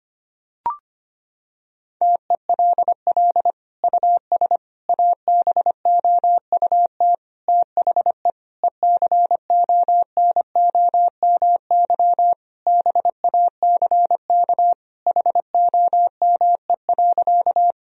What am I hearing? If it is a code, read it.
Morse code, heard as a keyed tone, about 25 wpm, 700 Hz. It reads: TELL US ABOUT THE ECONOMY BACK HOME.
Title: Sentences from Top 1000 Words - Review Part 3 - 25wpm